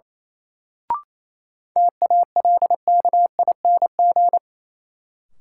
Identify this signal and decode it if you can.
Morse code, heard as a keyed tone, about 28 wpm, 700 Hz. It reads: TALKING
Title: Sets of 2 Words - New Word Drill - 28wpm